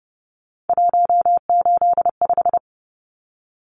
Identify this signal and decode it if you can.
Morse code, heard as a keyed tone, about 30 wpm, 700 Hz. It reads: 185
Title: Numbers - 3-Digit 30wpm